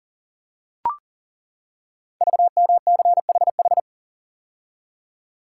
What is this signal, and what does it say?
Morse code, heard as a keyed tone, about 39 wpm, 690 Hz. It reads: VMCHH
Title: Code Groups - 5 Letter:Numbers - 40wpm